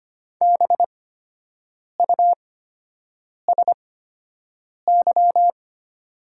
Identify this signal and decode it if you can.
Morse code, heard as a keyed tone, about 25 wpm, spaced as for 7 wpm, 700 Hz. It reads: BUSY
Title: 8x Character Spacing - Top 500 Words - 25wpm